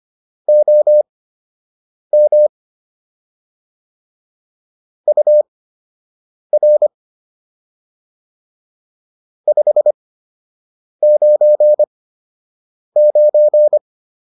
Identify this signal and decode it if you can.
Morse code, heard as a keyed tone, about 25 wpm, spaced as for 7 wpm, 600 Hz. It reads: OM UR 599